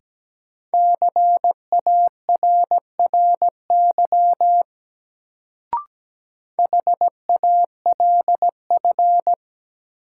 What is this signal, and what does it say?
Morse code, heard as a keyed tone, about 17 wpm, 700 Hz. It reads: CARRY HALF
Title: Sentences from Top 400 Words - New Word Drill - 17wpm